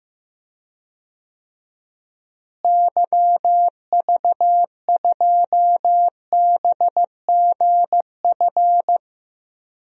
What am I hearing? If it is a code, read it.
Morse code, heard as a keyed tone, about 15 wpm, 700 Hz. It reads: YV2BGF
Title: DXCC-100 Call Signs - 6 Characters - Repeated 2x - 15wpm